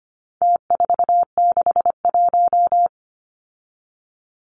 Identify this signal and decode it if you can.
Morse code, heard as a keyed tone, about 25 wpm, 700 Hz. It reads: T461
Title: Numbers - Contest Style - 4-Digit 25wpm